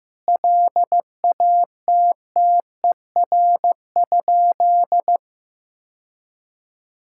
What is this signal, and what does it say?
Morse code, heard as a keyed tone, about 15 wpm, 700 Hz. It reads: LATTER?
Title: Sentences from Top 1600 Words 15wpm